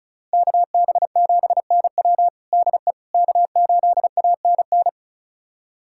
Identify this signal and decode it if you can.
Morse code, heard as a keyed tone, about 36 wpm, 700 Hz. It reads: KB7NW DE K8ANN
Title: USA Callsign DE Callsign - 5 Characters - 35wpm